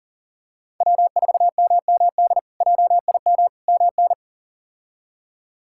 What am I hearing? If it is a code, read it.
Morse code, heard as a keyed tone, about 40 wpm, 700 Hz. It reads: W4MMD JIM MN
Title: K1USN SST QSO Practice - 40wpm